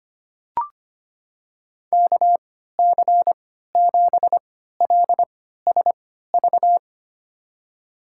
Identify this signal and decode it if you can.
Morse code, heard as a keyed tone, about 25 wpm, spaced as for 14 wpm, 700 Hz. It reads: KC7LSV
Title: USA Call Signs - 6 Characters - 3x Character Spacing - 25wpm